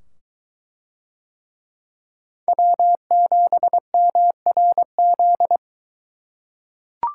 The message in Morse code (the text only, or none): W7MRZ